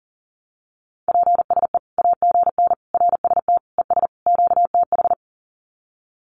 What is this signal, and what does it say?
Morse code, heard as a keyed tone, about 40 wpm, 700 Hz. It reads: PSE AGN RST ES QTH
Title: Common QSO Phrases 40wpm